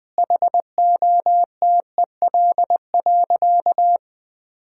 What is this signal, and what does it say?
Morse code, heard as a keyed tone, about 21 wpm, 700 Hz. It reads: HOTEL.
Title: Sentences from Top 1600 Words 20wpm